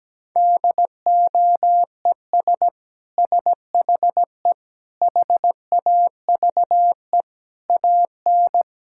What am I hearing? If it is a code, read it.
Morse code, heard as a keyed tone, about 17 wpm, 700 Hz. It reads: DOES SHE HAVE AN
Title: Sentences from Top 1000 Words - Review Part 6 - 17wpm